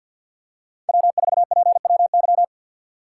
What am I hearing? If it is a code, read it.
Morse code, heard as a keyed tone, about 46 wpm, 700 Hz. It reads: W4PWY
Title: USA Call Signs - 5 Characters - 4x Character Spacing - 50wpm